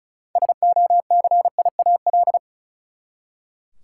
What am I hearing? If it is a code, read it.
Morse code, heard as a keyed tone, about 36 wpm, 700 Hz. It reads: SOCIAL
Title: K1USN SST QSO Practice 35wpm